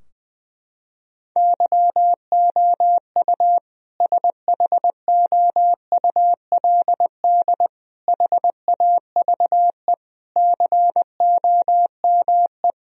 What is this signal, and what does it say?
Morse code, heard as a keyed tone, about 20 wpm, 700 Hz. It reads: YOU SHOULD HAVE COME
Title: Sentences from Top 1000 Words - Review Part 4 - 20wpm